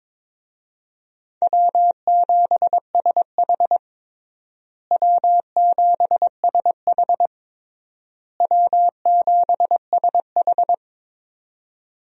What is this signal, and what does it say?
Morse code, heard as a keyed tone, about 22 wpm, 700 Hz. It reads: W7SH W7SH W7SH
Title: USA Call Signs - 4 Characters - Repeated 3x - 22wpm